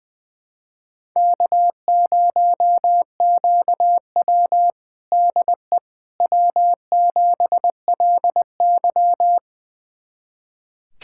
Morse code, heard as a keyed tone, about 20 wpm, 700 Hz. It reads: K0QW DE W7LY
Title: USA Callsign DE Callsign - 4 Characters - 20wpm